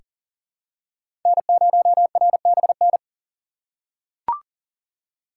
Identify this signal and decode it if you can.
Morse code, heard as a keyed tone, about 40 wpm, 700 Hz. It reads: N0RBN
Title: USA Call Signs - 2x Character Spacing - 40wpm